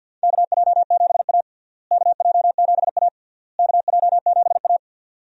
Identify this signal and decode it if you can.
Morse code, heard as a keyed tone, about 52 wpm, 700 Hz. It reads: KJ7A KJ7A KJ7A
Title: USA Call Signs - 4 Characters - Repeated 3x - 50wpm